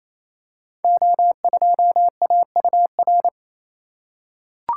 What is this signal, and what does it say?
Morse code, heard as a keyed tone, about 27 wpm, 700 Hz. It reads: O2AUR
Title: Code Groups - 5 Letter:Numbers - 28wpm